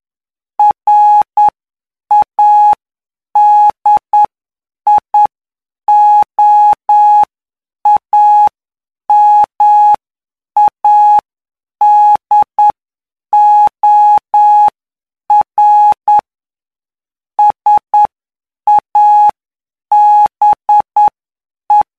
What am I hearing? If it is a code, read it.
Morse code, heard as a keyed tone, about 9 wpm, spaced as for 7 wpm, 840 Hz. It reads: RADIOAMADOR SABE
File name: cw / py2lcd / 19-PY2LCD-Li��o 28